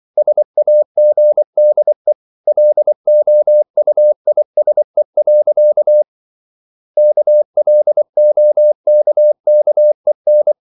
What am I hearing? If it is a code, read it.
Morse code, heard as a keyed tone, about 24 wpm, 600 Hz. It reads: SAGDE LOUISE. KLOKKEN
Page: Ferie 24 wpm